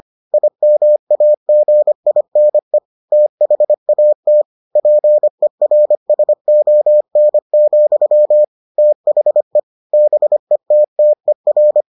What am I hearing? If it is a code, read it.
Morse code, heard as a keyed tone, about 25 wpm, 600 Hz. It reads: IMAGINE THAT PERSON, THE BETTER